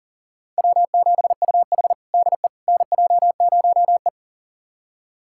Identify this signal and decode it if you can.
Morse code, heard as a keyed tone, about 40 wpm, 700 Hz. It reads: W7UH DE NJ0E
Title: USA Callsign DE Callsign - 4 Characters - 40wpm